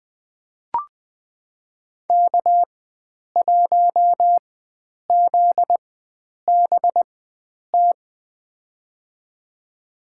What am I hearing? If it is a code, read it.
Morse code, heard as a keyed tone, about 20 wpm, spaced as for 9 wpm, 700 Hz. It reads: K1ZBT
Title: USA Call Signs - 5 Characters - 4x Character Spacing - 20wpm